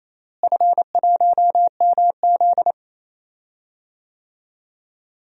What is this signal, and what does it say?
Morse code, heard as a keyed tone, about 28 wpm, 700 Hz. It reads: F1MZ
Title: DXCC-100 Call Signs - 4 Characters - Repeated 3x - 28wpm